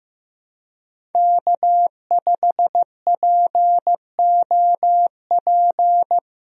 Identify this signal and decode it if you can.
Morse code, heard as a keyed tone, about 16 wpm, 700 Hz. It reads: K5POP
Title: USA Call Signs - 5 Characters - 15wpm